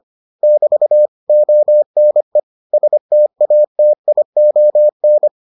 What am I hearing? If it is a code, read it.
Morse code, heard as a keyed tone, about 25 wpm, 600 Hz. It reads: <BT> ONE STATION